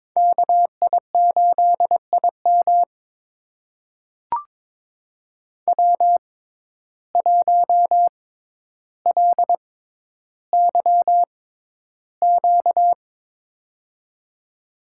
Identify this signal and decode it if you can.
Morse code, heard as a keyed tone, about 22 wpm, spaced as for 11 wpm, 700 Hz. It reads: KI8IM W1LYQ
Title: USA Call Signs - 5 Characters - 6x Character Spacing - 22wpm